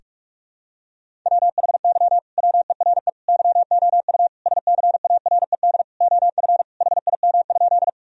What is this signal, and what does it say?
Morse code, heard as a keyed tone, about 46 wpm, 700 Hz. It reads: WHY WERE YOU SCARED OF HIM?
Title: Sentences from Top 1000 Words - Review Part 4 - 45wpm